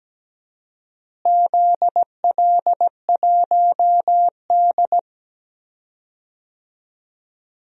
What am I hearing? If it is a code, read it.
Morse code, heard as a keyed tone, about 17 wpm, 700 Hz. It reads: ZL1D
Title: DXCC-100 Call Signs - 4 Characters - Repeated 3x - 17wpm